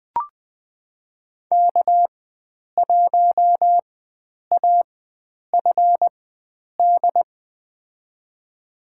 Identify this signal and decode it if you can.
Morse code, heard as a keyed tone, about 20 wpm, spaced as for 9 wpm, 700 Hz. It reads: K1AFD
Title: USA Call Signs - 4x Character Spacing - 20wpm